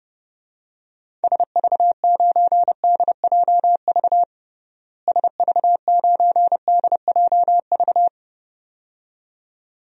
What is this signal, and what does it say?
Morse code, heard as a keyed tone, about 31 wpm, 700 Hz. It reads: SV9DJV SV9DJV
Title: DXCC-100 Call Signs - 6 Characters - Repeated 2x - 30wpm